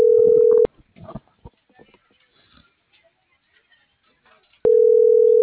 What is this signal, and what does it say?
Ringback tone (North America), 440 + 480 Hz.